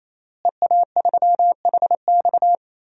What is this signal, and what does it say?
Morse code, heard as a keyed tone, about 28 wpm, 700 Hz. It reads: EA3HX
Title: DXCC-100 Call Signs - 5 Characters - Repeated 3x - 28wpm